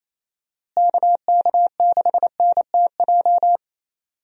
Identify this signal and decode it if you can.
Morse code, heard as a keyed tone, about 28 wpm, 700 Hz. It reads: KK6NTJ
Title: USA Call Signs - 6 Characters - Repeated 3x - 28wpm